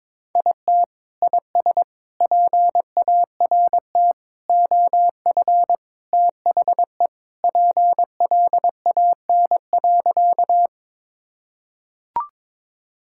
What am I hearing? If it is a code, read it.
Morse code, heard as a keyed tone, about 22 wpm, 700 Hz. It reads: IT IS PART OF THE PLAN.
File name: practice / 22wpm / Sentences with 4-Word Phrases - 22wpm